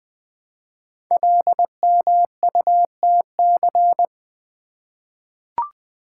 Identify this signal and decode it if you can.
Morse code, heard as a keyed tone, about 20 wpm, 700 Hz. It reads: LMUTC
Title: Code Groups - 5 Letters - 20wpm